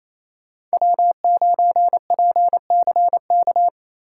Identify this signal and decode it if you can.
Morse code, heard as a keyed tone, about 37 wpm, 700 Hz. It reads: W9PCK